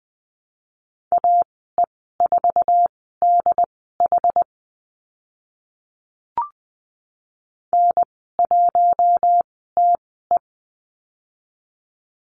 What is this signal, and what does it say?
Morse code, heard as a keyed tone, about 20 wpm, spaced as for 14 wpm, 700 Hz. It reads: AE4DH N1TE
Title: 082 Lesson 14g - Number 2 - Call Signs 20wpm